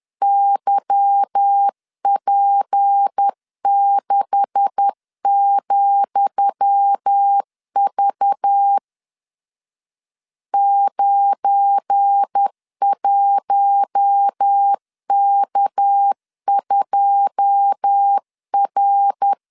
Morse code, heard as a keyed tone, about 10 wpm, 800 Hz. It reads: YP6,V 91K2R
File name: LEZIONI / 10 SIGNS / 10D COMA 10wpm 3013